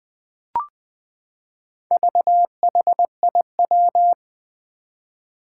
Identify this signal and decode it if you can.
Morse code, heard as a keyed tone, about 20 wpm, 700 Hz. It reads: VHIW
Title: Four Challenging Letter-Number Combinations 20wpm